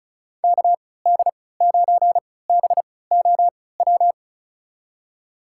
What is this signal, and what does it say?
Morse code, heard as a keyed tone, about 36 wpm, spaced as for 20 wpm, 700 Hz. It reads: KD9BOW
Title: USA Call Signs - 6 Characters - 3x Character Spacing - 35wpm